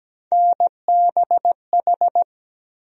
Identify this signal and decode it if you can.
Morse code, heard as a keyed tone, about 17 wpm, 700 Hz. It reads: NBH
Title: USA Call Sign Postfixes - 3 Letters 17wpm